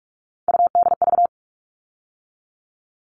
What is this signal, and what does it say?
Morse code, heard as a keyed tone, about 44 wpm, 700 Hz. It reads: UDV
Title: All Cut Numbers - 3-Digit 45wpm